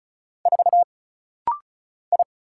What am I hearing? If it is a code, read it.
Morse code, heard as a keyed tone, about 35 wpm, 700 Hz. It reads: 4 I
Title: Single Letter-Number Alt - Rapid-Fire 35wpm